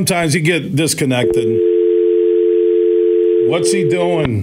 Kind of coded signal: telephone call progress tone